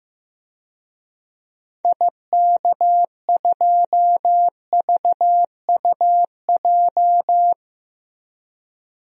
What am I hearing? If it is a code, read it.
Morse code, heard as a keyed tone, about 15 wpm, 700 Hz. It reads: IK2VUJ
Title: DXCC-100 Call Signs - 6 Characters - Repeated 3x - 15wpm